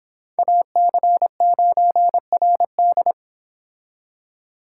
Morse code, heard as a keyed tone, about 25 wpm, 700 Hz. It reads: AC9RD